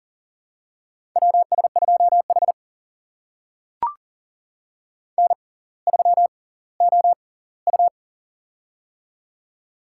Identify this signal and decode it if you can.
Morse code, heard as a keyed tone, about 40 wpm, spaced as for 20 wpm, 700 Hz. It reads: WS2H N3OU
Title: USA Call Signs - 6x Character Spacing - 40wpm